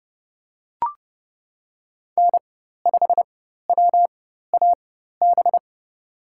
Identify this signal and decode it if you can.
Morse code, heard as a keyed tone, about 30 wpm, spaced as for 14 wpm, 700 Hz. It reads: N5WAB
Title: USA Call Signs - 5 Characters - 4x Character Spacing - 30wpm